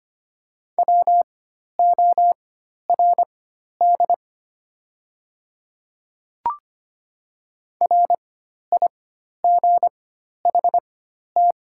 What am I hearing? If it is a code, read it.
Morse code, heard as a keyed tone, about 25 wpm, spaced as for 12 wpm, 700 Hz. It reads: WORD RIGHT
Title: 4x Character Spacing - Top 500 Words - 25wpm